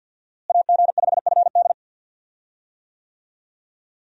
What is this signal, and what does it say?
Morse code, heard as a keyed tone, about 48 wpm, 700 Hz. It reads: AG5FD
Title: USA Call Signs - 5 Characters - Repeated 2x - 50wpm